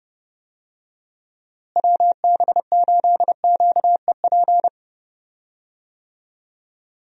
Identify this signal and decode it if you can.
Morse code, heard as a keyed tone, about 30 wpm, 700 Hz. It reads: WB8QEP
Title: USA Call Signs - 6 Characters - Repeated 2x - 30wpm